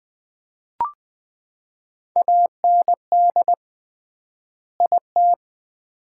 Morse code, heard as a keyed tone, about 20 wpm, 700 Hz. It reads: AND IT
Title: 3x Word-Spacing - Sets of 2 Words - 20wpm